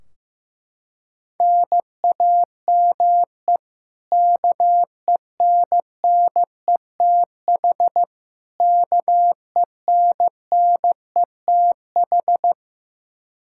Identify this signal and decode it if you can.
Morse code, heard as a keyed tone, about 15 wpm, 700 Hz. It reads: NAME KENNETH KENNETH
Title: Practice QSO - Exchange 1 - Beginning - 15wpm